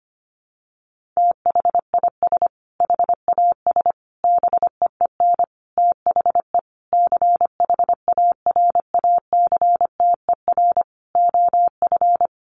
Morse code, heard as a keyed tone, about 25 wpm, 700 Hz. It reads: THIS HAS BEEN THE CHARACTER OF
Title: Sentences from Top 1400 Words 25wpm